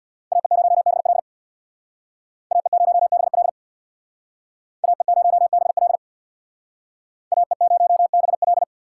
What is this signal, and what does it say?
Morse code, heard as a keyed tone, about 62 wpm, 700 Hz. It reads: AE0BL AE0BL AE0BL AE0BL
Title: Speed Racing - USA 2x2 Call Signs 50wpm